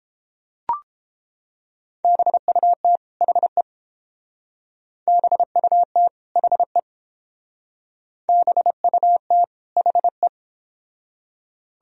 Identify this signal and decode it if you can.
Morse code, heard as a keyed tone, about 30 wpm, 700 Hz. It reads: BUT HE BUT HE BUT HE